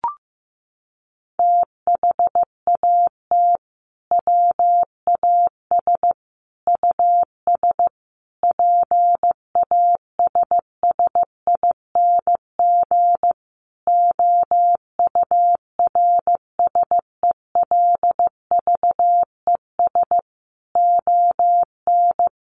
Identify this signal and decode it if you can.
Morse code, heard as a keyed tone, about 15 wpm, 700 Hz. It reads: THAT WAS US PASSING OURSELVES ON
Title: A Sound Of Thunder 15wpm